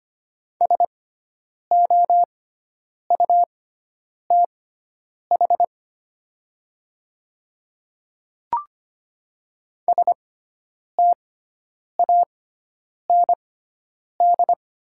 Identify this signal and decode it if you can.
Morse code, heard as a keyed tone, about 25 wpm, spaced as for 9 wpm, 700 Hz. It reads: SOUTH STAND
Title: 6x Character Spacing - Top 500 Words - 25wpm